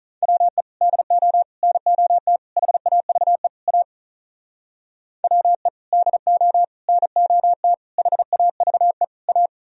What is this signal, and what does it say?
Morse code, heard as a keyed tone, about 44 wpm, 700 Hz. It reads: WE DO NOT HAVE A WE DO NOT HAVE A
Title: Speed-Racing - Sets of 5 Words 35wpm